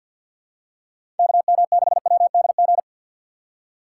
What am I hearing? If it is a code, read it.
Morse code, heard as a keyed tone, about 48 wpm, 700 Hz. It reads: KM6WDG
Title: USA Call Signs - 6 Characters - 4x Character Spacing - 50wpm